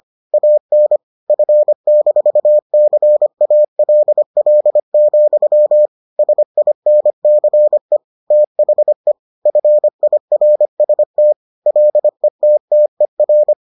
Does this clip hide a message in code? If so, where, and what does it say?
Yes: Morse code from 0.3 to 13.5 s, heard as a keyed tone, about 25 wpm, 600 Hz. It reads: AN F-CALL, SINCE THE FIRST LETTER